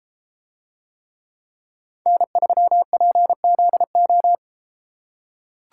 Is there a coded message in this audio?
Yes: Morse code from 2.1 to 4.4 s, heard as a keyed tone, about 32 wpm, 700 Hz. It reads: N3PZO